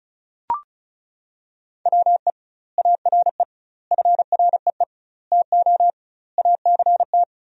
Morse code, heard as a keyed tone, about 36 wpm, 700 Hz. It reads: WE ARE FREE TO ACT
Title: 2x Word-Spacing - Sentences from Top 500 Words - 35wpm